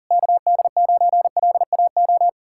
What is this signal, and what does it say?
Morse code, heard as a keyed tone, about 52 wpm, 700 Hz. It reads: KD9LAO